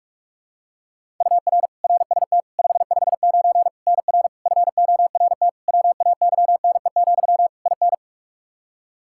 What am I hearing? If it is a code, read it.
Morse code, heard as a keyed tone, about 46 wpm, 700 Hz. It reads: UR RST 559 NR FORT WAYNE, IN
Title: Practice QSO - Exchange 1 - Beginning - 45wpm